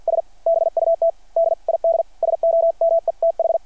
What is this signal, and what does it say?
Morse code, heard as a keyed tone, about 52 wpm, spaced as for 45 wpm, 660 Hz. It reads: S BUT DID SOMETH
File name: archived / CRSnet